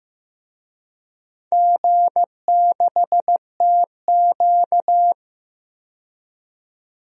Morse code, heard as a keyed tone, about 15 wpm, 700 Hz. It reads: G6TQ
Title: DXCC-100 Call Signs - 4 Characters - 15wpm